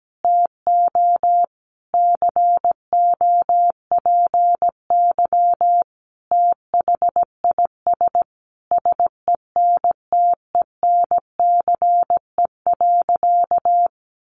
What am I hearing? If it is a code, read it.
Morse code, heard as a keyed tone, about 17 wpm, 700 Hz. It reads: TO COPY THIS SENTENCE.
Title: Additional Easy Phrases 17wpm